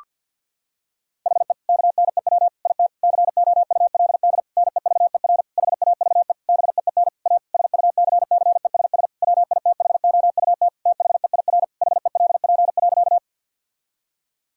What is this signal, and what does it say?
Morse code, heard as a keyed tone, about 50 wpm, 700 Hz. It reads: HE KNEW IT COULD NEVER HAVE BEEN A SUCCESS WITHOUT THEIR HELP.
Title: Sentences from Top 1000 Words - Review Part 6 - 50wpm